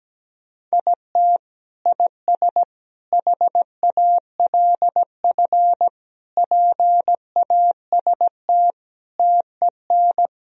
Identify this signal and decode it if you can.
Morse code, heard as a keyed tone, about 17 wpm, 700 Hz. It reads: IT IS HALF PAST TEN
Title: Sentences from Top 1000 Words - Review Part 3 - 17wpm